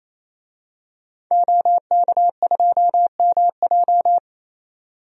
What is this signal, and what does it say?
Morse code, heard as a keyed tone, about 28 wpm, 700 Hz. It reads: OK2MJ
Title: DXCC-100 Call Signs - 5 Characters - 28wpm